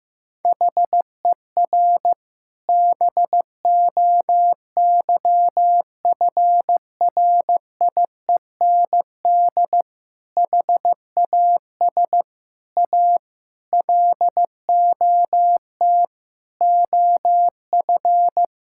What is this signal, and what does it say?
Morse code, heard as a keyed tone, about 15 wpm, 700 Hz. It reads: HER BOYFRIEND HAS A LOT OF
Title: Sentences from Top 1000 Words - Review Part 5 - 15wpm